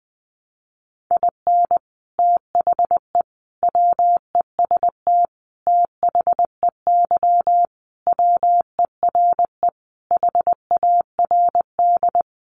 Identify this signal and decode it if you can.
Morse code, heard as a keyed tone, about 20 wpm, 700 Hz. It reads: IN THE WEST THEY WERE HARD
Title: Sentences from Top 1300 Words 20wpm